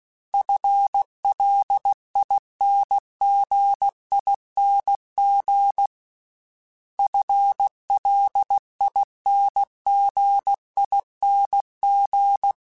Morse code, heard as a keyed tone, about 16 wpm, 800 Hz. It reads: FLINGING FLINGING
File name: FL / FL mp3s / FL Prefixes 3x2 @15